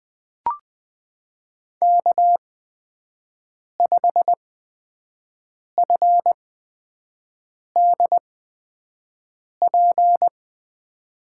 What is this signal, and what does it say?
Morse code, heard as a keyed tone, about 20 wpm, spaced as for 5 wpm, 700 Hz. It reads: K5FDP
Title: USA Call Signs - 5 Characters - 8x Character Spacing - 20wpm